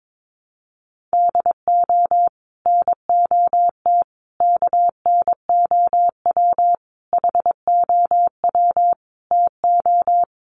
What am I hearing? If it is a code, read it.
Morse code, heard as a keyed tone, about 22 wpm, 700 Hz. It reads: DO NOT KNOW HOW TO
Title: Sets of 5 Words 22wpm